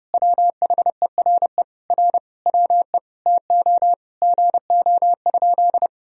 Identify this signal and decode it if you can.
Morse code, heard as a keyed tone, about 32 wpm, 700 Hz. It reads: WHERE R WE TO GO?
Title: Sentences from Top 200 Words and Abbreviations 30wpm